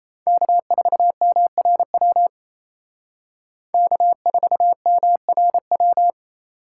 Morse code, heard as a keyed tone, about 33 wpm, 700 Hz. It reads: K4MRW K4MRW
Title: Speed Racing - USA Call Signs 28wpm